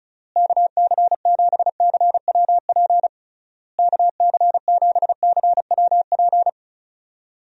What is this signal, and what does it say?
Morse code, heard as a keyed tone, about 34 wpm, 700 Hz. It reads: KC7CWP KC7CWP
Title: USA Call Signs - 6 Characters - Repeated 3x - 35wpm